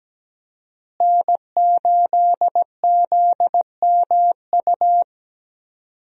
Morse code, heard as a keyed tone, about 17 wpm, 700 Hz. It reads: N8ZMU
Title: USA Call Signs - 5 Characters - Repeated 3x - 17wpm